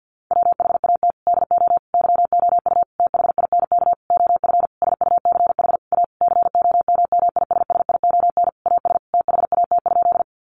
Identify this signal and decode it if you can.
Morse code, heard as a keyed tone, about 50 wpm, 700 Hz. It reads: WHAT DO YOU THINK OF SUCH A COMMISSION AS THAT?
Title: Sentences from Top 1400 Words 50wpm